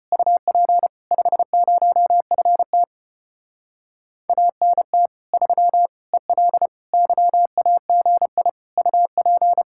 Morse code, heard as a keyed tone, about 33 wpm, 700 Hz. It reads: UP 50FT ANT 3 EL YAGI UP